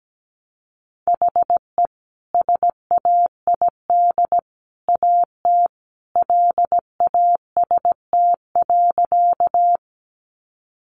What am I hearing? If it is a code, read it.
Morse code, heard as a keyed tone, about 17 wpm, 700 Hz. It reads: HE SAID AT LAST.
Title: Sentences from Top 300 Words 17wpm